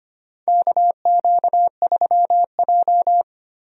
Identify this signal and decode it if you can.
Morse code, heard as a keyed tone, about 25 wpm, 700 Hz. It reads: KQ3J
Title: USA Call Signs - 4 Characters - Repeated 3x - 25wpm